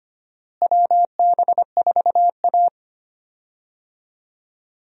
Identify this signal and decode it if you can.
Morse code, heard as a keyed tone, about 25 wpm, 700 Hz. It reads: WB4A
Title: USA Call Signs - 4 Characters - Repeated 3x - 25wpm